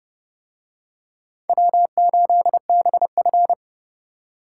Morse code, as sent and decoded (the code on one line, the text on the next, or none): .-- ---.. -... ..-.
W8BF